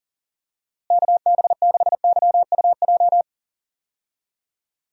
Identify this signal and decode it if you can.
Morse code, heard as a keyed tone, about 40 wpm, 700 Hz. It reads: KB6YUJ
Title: USA Call Signs - 6 Characters - Repeated 2x - 40wpm